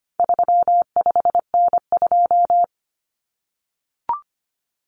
Morse code, heard as a keyed tone, about 25 wpm, 700 Hz. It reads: 35N2
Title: Common Cut Numbers - 4-Digit 25wpm